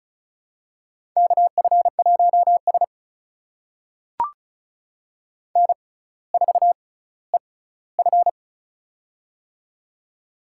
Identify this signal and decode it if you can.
Morse code, heard as a keyed tone, about 36 wpm, spaced as for 18 wpm, 700 Hz. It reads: KF1S N4EF
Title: USA Call Signs - 4 Characters - 6x Character Spacing - 35wpm